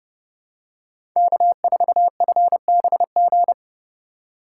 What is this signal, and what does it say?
Morse code, heard as a keyed tone, about 30 wpm, 700 Hz. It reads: K4FBG
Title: USA Call Signs - 5 Characters - 2x Character Spacing - 30wpm